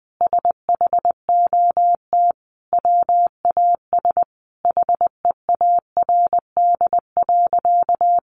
Morse code, heard as a keyed tone, about 20 wpm, 700 Hz. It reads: SHOT WAS HEARD.
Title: Sentences from Top 1100 Words 20wpm